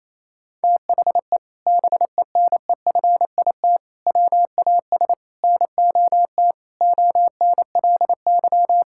Morse code, heard as a keyed tone, about 28 wpm, 700 Hz. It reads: THE BENEFIT WAS NOT ONLY
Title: Sentences from Top 1000 Words - Review Part 6 - 28wpm